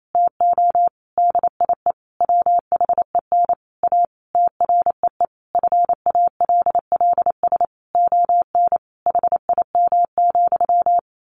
Morse code, heard as a keyed tone, about 28 wpm, 700 Hz. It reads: TO DIE WHEN A TREE FALLS ON HIM,
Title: A Sound Of Thunder 28wpm